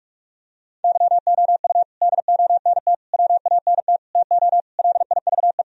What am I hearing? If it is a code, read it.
Morse code, heard as a keyed tone, about 46 wpm, 700 Hz. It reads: YOU DONT WANT TO LIVE
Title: Sentences from Top 1000 Words - Review Part 1 - 45wpm